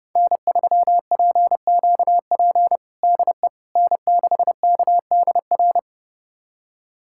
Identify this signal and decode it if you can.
Morse code, heard as a keyed tone, about 30 wpm, 700 Hz. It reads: N3PQP DE N6KDR
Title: USA Callsign DE Callsign - 5 Characters - 30wpm